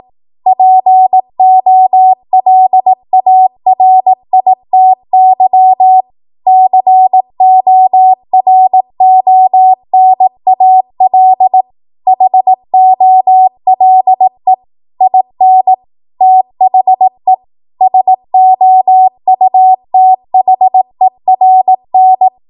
Morse code, heard as a keyed tone, about 18 wpm, 750 Hz. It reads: POLARITY CORONAL HOLE IN THE SOUTHERN
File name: Morse / 18WPM